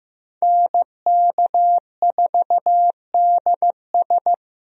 Morse code, heard as a keyed tone, about 15 wpm, 700 Hz. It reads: NK4DS
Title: USA Call Signs - 5 Characters - Repeated 2x - 15wpm